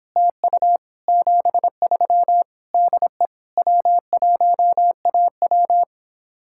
Morse code, heard as a keyed tone, about 29 wpm, 700 Hz. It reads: TU 73 DE W1AW